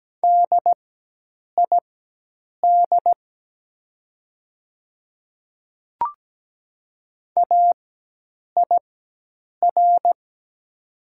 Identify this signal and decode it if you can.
Morse code, heard as a keyed tone, about 17 wpm, spaced as for 8 wpm, 690 Hz. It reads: DID AIR
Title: 4x Character Spacing - Top 300 Words - 17wpm